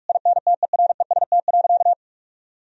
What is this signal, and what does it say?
Morse code, heard as a keyed tone, about 46 wpm, 700 Hz. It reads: INTEREST.
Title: Sentences from Top 1000 Words - Review Part 4 - 45wpm